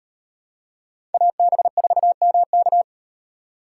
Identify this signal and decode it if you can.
Morse code, heard as a keyed tone, about 36 wpm, 700 Hz. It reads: AB4MK